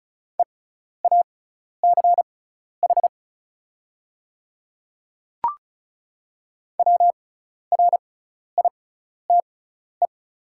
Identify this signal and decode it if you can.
Morse code, heard as a keyed tone, about 34 wpm, spaced as for 12 wpm, 700 Hz. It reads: EACH WRITE
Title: 6x Character Spacing - Top 100 Words - 35wpm